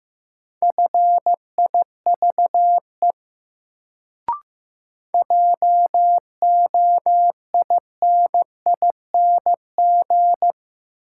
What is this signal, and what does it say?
Morse code, heard as a keyed tone, about 15 wpm, 700 Hz. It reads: FIVE JOINING
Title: Sets of 5 Words - New Word Drill - 15wpm